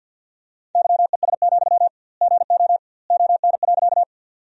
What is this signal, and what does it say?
Morse code, heard as a keyed tone, about 52 wpm, 700 Hz. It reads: YES, GO ON.
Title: 2x Word-Spacing - Sentences from Top 500 Words - 50wpm